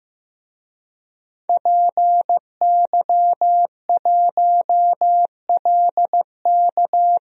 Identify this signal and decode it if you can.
Morse code, heard as a keyed tone, about 15 wpm, 700 Hz. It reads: PY1LK
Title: DXCC-100 Call Signs - 5 Characters - 15wpm